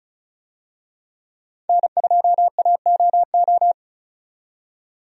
Morse code, heard as a keyed tone, about 36 wpm, 700 Hz. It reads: N2AOO
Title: USA Call Signs - 5 Characters - 35wpm